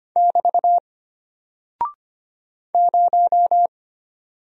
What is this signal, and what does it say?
Morse code, heard as a keyed tone, about 25 wpm, 700 Hz. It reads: <BT> 0
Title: Single Character (Letter:Number:Punctuation:Prosign) Alt 25wpm